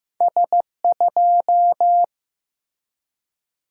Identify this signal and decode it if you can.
Morse code, heard as a keyed tone, about 15 wpm, 700 Hz. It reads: S2
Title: Single-Letter Single-Number 15wpm